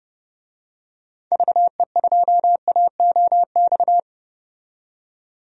Morse code, heard as a keyed tone, about 30 wpm, 700 Hz. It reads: VE2AOX
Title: DXCC-100 Call Signs - 6 Characters - 30wpm